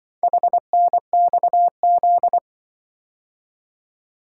Morse code, heard as a keyed tone, about 24 wpm, 700 Hz. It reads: HNXZ
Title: Speed-Racing - Four Letter-Number Combinations - 20wpm